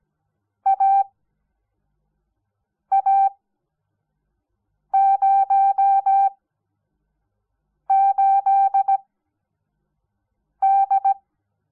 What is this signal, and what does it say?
Morse code, heard as a keyed tone, about 17 wpm, spaced as for 5 wpm, 780 Hz. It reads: AA08D
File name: Onlinelehrgang / morsen / DJ4UF Morsen06